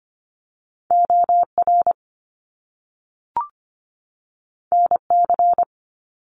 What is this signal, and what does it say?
Morse code, heard as a keyed tone, about 25 wpm, 700 Hz. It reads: OR NC
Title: Top Two Letter Combinations in Words 25wpm